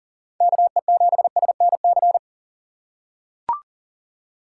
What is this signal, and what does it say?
Morse code, heard as a keyed tone, about 40 wpm, 700 Hz. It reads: KE7SNC
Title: USA Call Signs - 4x Character Spacing - 40wpm